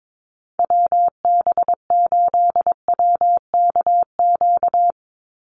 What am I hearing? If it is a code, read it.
Morse code, heard as a keyed tone, about 22 wpm, 700 Hz. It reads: WB8WKQ
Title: Call Signs - 22wpm